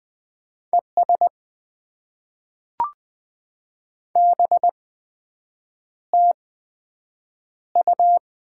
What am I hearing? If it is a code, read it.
Morse code, heard as a keyed tone, about 20 wpm, spaced as for 7 wpm, 700 Hz. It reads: ES BTU
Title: Top 20 QSO Elements - 8x Character Spacing - 20wpm